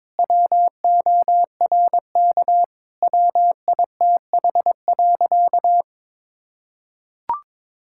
Morse code, heard as a keyed tone, about 22 wpm, 700 Hz. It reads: WORK WITH.
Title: Sentences from Top 1000 Words - Review Part 6 - 22wpm